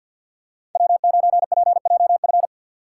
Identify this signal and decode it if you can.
Morse code, heard as a keyed tone, about 52 wpm, 700 Hz. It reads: W9PJF
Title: USA Call Signs - 5 Characters - 8x Character Spacing - 50wpm